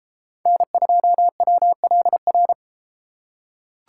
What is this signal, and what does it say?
Morse code, heard as a keyed tone, about 31 wpm, 700 Hz. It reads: N2WLR